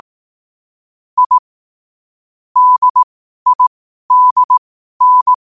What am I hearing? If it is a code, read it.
Morse code, heard as a keyed tone, about 18 wpm, spaced as for 12 wpm, 1000 Hz. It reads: I DIDN
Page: Quotes for Fri, 15 Aug 2025 in Morse Code at 12 words per minute.